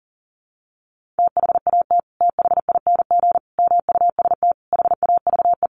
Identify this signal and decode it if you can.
Morse code, heard as a keyed tone, about 40 wpm, 700 Hz. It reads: THAT THING MUST HAVE
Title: Sentences from Top 1500 Words 40wpm